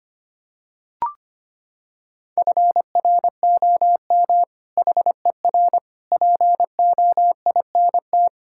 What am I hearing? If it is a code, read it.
Morse code, heard as a keyed tone, about 25 wpm, 700 Hz. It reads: FROM HER POINT
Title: Sentences from Top 1000 Words - Review Part 6 - 25wpm